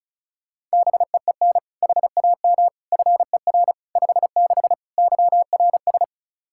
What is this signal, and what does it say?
Morse code, heard as a keyed tone, about 34 wpm, 700 Hz. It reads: BEEN HAM FER 56 YRS
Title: Practice QSO - Exchange 3 - More Info - 35wpm